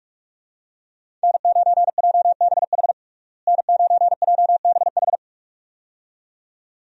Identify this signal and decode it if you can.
Morse code, heard as a keyed tone, about 41 wpm, 700 Hz. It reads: N9JBH N9JBH
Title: USA Call Signs - 5 Characters - Repeated 2x - 45wpm